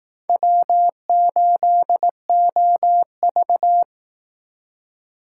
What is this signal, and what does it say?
Morse code, heard as a keyed tone, about 18 wpm, 700 Hz. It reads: W8OV
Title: Speed Racing - USA 1x2 Call Signs 15wpm